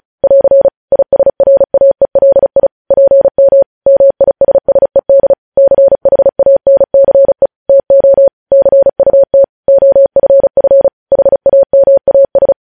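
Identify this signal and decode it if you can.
Morse code, heard as a keyed tone, about 36 wpm, 550 Hz. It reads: <AR> ISRAELI PM MISSED CHANCE TO CUT OFF HAMAS
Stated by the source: Morse code news headlines sent at 35 wpm.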